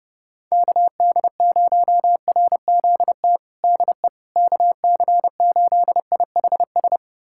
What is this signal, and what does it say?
Morse code, heard as a keyed tone, about 32 wpm, 700 Hz. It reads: KD0RZT DE KC8IHS
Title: USA Callsign DE Callsign 30wpm